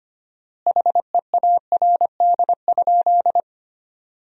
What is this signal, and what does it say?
Morse code, heard as a keyed tone, about 25 wpm, 700 Hz. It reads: HEARD?
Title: 4x Word-Spacing - Sentences from Top 400 Words - 25wpm